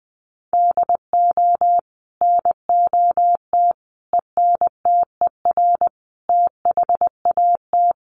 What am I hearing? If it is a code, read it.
Morse code, heard as a keyed tone, about 20 wpm, 700 Hz. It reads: DO NOT ENTER THAT
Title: Sentences from Top 1300 Words 20wpm